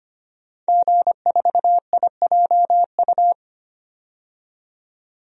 Morse code, heard as a keyed tone, about 25 wpm, 700 Hz. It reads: G4IJU
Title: DXCC-100 Call Signs - 5 Characters - 25wpm